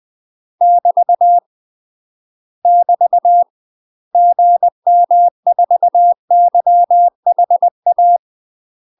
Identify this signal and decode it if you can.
Morse code, heard as a keyed tone, about 20 wpm, 700 Hz. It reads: <BT> <BT> GM4YHA
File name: UK MASTER QSOs TRAINING/20WPM